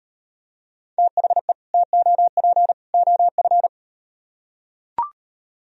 Morse code, heard as a keyed tone, about 37 wpm, 700 Hz. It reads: THE TOP OF